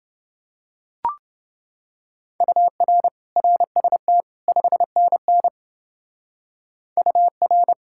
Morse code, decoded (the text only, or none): UR RST 5NN UR